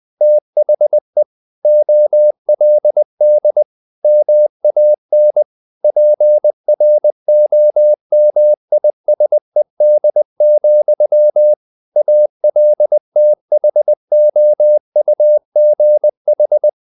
Morse code, heard as a keyed tone, about 20 wpm, 600 Hz. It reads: THE OLD MAN PROMISED, ALTHOUGH
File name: the old man and the sea part 2 20wpm